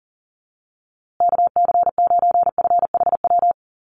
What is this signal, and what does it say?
Morse code, heard as a keyed tone, about 40 wpm, 700 Hz. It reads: KC9FHW
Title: USA Call Signs Repeated 3x 40wpm